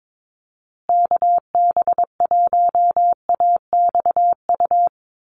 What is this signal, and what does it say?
Morse code, heard as a keyed tone, about 22 wpm, 700 Hz. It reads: KB1AXU
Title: USA Call Signs Repeated 3x 22wpm